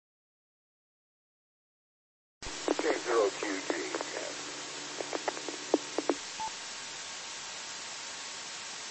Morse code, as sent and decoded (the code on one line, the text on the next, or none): none